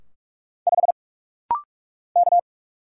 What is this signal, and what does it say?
Morse code, heard as a keyed tone, about 46 wpm, 700 Hz. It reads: 5 K
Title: Single Letter-Number Alt - Rapid-Fire 45wpm